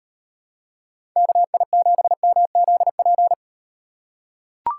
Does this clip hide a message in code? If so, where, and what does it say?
Yes: Morse code from 1.2 to 3.3 s, heard as a keyed tone, about 38 wpm, 700 Hz. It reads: KI7MZP